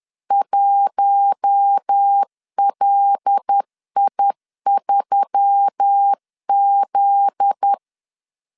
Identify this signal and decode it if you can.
Morse code, heard as a keyed tone, about 10 wpm, 800 Hz. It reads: 1LI3Z
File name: LEZIONI / 7 GF9Z / 7 GF9Z 10wpm 2611